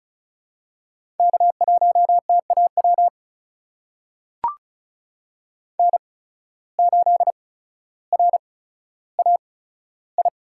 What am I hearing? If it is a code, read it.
Morse code, heard as a keyed tone, about 34 wpm, spaced as for 15 wpm, 700 Hz. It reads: K1TAW N8RAI
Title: USA Call Signs - 5 Characters - 8x Character Spacing - 35wpm